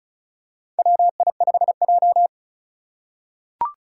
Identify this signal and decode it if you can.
Morse code, heard as a keyed tone, about 36 wpm, 700 Hz. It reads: WI5J
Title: Four Challenging Letter-Number Combinations 35wpm